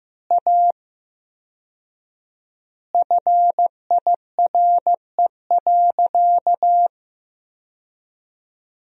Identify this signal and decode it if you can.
Morse code, heard as a keyed tone, about 15 wpm, 700 Hz. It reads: A FIRE.
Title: 4x Word-Spacing - Sentences from Top 400 Words - 15wpm